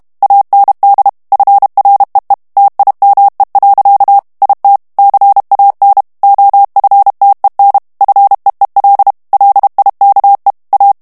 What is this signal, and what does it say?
Morse code, heard as a keyed tone, about 32 wpm, 800 Hz. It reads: AND FREE TIME. IT CAN OFTEN FEEL LIKE A
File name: morse mp3 / ANGLAIS texte clair 1 vitesse 30 mots minute